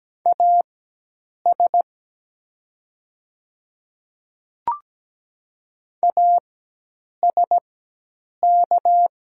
Morse code, heard as a keyed tone, about 17 wpm, spaced as for 8 wpm, 700 Hz. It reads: AS ASK
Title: 4x Character Spacing - Top 200 Words - 17wpm